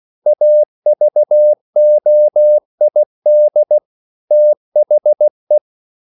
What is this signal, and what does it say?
Morse code, heard as a keyed tone, about 16 wpm, 600 Hz. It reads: AVOID THE
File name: CW / Practice / EnglishQuotations0116